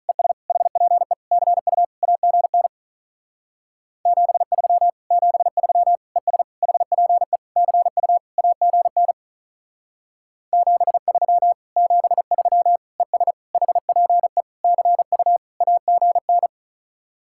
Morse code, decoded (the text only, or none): ES HPE CU AGN 73 73 ES HPE CU AGN 73 73 ES HPE CU AGN